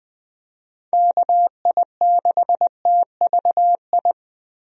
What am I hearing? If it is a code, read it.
Morse code, heard as a keyed tone, about 20 wpm, 700 Hz. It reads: KI6TVI
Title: USA Call Signs - 6 Characters - 20wpm